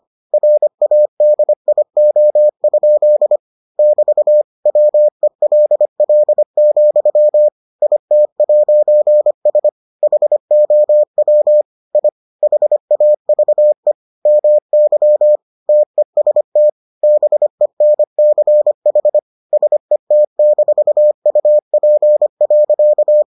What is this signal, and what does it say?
Morse code, heard as a keyed tone, about 25 wpm, 600 Hz. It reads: RADIO? <BT> WELL, IT'S HOW I HAVE MY TEST BENCH SET-UP.